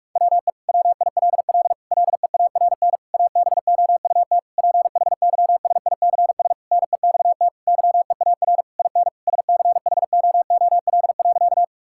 Morse code, heard as a keyed tone, about 42 wpm, 700 Hz. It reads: WE WILL LEARN ABOUT PHYSICS NEXT YEAR IN SCHOOL.
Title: Sentences from Top 1000 Words - Review Part 5 - 45wpm